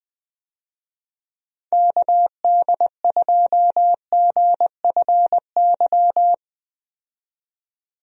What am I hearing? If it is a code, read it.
Morse code, heard as a keyed tone, about 20 wpm, 700 Hz. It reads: KD2GFY
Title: USA Call Signs - 6 Characters - 20wpm